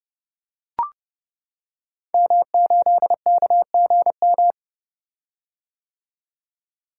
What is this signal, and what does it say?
Morse code, heard as a keyed tone, about 30 wpm, 700 Hz. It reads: M8KGM
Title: Code Groups - 5 Letter:Numbers - 30wpm